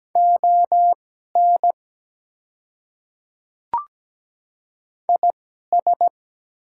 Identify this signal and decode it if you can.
Morse code, heard as a keyed tone, about 17 wpm, spaced as for 12 wpm, 700 Hz. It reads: ON IS
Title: 2x Character Spacing - Top 100 Words - 17wpm